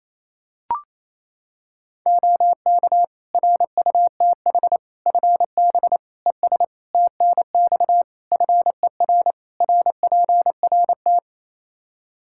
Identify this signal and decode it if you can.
Morse code, heard as a keyed tone, about 28 wpm, 700 Hz. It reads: OK RUTH FB ES TNX FER RPRT
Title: Practice QSO - Exchange 2 - Info - 28wpm